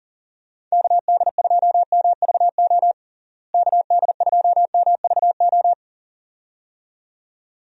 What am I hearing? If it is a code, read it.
Morse code, heard as a keyed tone, about 40 wpm, 700 Hz. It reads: KD2MVO KD2MVO
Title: USA Call Signs - 6 Characters - Repeated 2x - 40wpm